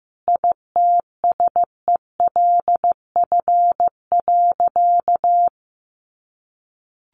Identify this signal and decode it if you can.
Morse code, heard as a keyed tone, about 15 wpm, 700 Hz. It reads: ITSELF.
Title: Sentences from Top 1400 Words 15wpm